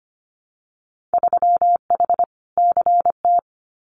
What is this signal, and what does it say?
Morse code, heard as a keyed tone, about 25 wpm, 700 Hz. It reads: 3H CT
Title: Winter Field Day Exchanges Encore 25wpm